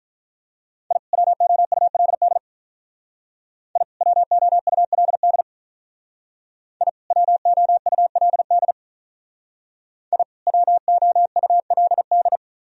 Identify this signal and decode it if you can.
Morse code, heard as a keyed tone, about 43 wpm, 700 Hz. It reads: I WOULD I WOULD I WOULD I WOULD
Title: Speed-Racing - Sets of 2 Words 35wpm